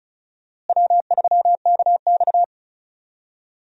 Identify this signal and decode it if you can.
Morse code, heard as a keyed tone, about 34 wpm, 700 Hz. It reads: W3KX
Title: USA Call Signs - 4 Characters - Repeated 2x - 35wpm